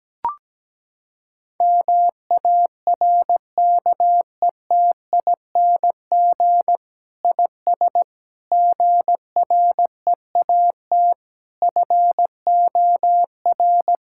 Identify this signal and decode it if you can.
Morse code, heard as a keyed tone, about 17 wpm, 700 Hz. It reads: MARKETING IS GREAT FOR
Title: Sentences from Top 1000 Words - Review Part 5 - 17wpm